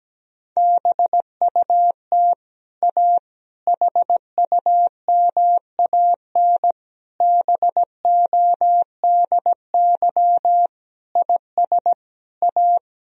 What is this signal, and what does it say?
Morse code, heard as a keyed tone, about 17 wpm, 700 Hz. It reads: BUT A HUMAN BODY IS A
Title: Sentences from Top 1000 Words - Review Part 3 - 17wpm